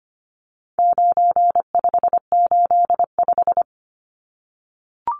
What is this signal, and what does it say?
Morse code, heard as a keyed tone, about 25 wpm, 700 Hz. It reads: 9585
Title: Numbers - 4-Digit 25wpm